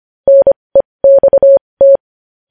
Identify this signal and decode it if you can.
Morse code, heard as a keyed tone, about 25 wpm, spaced as for 20 wpm, 550 Hz. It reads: NEXT
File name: podcast / News headlines-1x-25x20